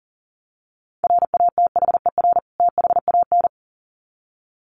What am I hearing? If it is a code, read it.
Morse code, heard as a keyed tone, about 40 wpm, 700 Hz. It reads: RATHER THAN
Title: Sets of 2 Words 40wpm